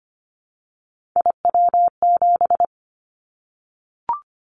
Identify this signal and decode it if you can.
Morse code, heard as a keyed tone, about 25 wpm, 700 Hz. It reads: IW7
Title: Callsign Prefixes for Top 25 DXCC - 25wpm